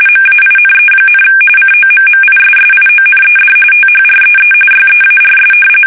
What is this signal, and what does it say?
RTTY, about 45.45 baud. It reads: MAINE, COMMISSIONED THREE YEARS
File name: fsk850 / battleships